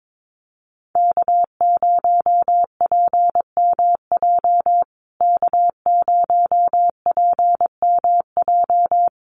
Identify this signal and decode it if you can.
Morse code, heard as a keyed tone, about 22 wpm, 700 Hz. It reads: K0PMJ K0PMJ
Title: USA Call Signs Repeated 3x 22wpm